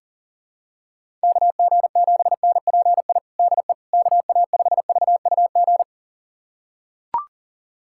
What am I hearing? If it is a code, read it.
Morse code, heard as a keyed tone, about 40 wpm, 700 Hz. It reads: KG7NPI DE KA5VUG
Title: USA Callsign DE Callsign - 6 Characters - 40wpm